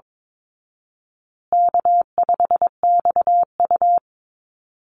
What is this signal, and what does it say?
Morse code, heard as a keyed tone, about 22 wpm, 700 Hz. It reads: K5XU
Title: Easy Call Signs Repeated 3x - 22wpm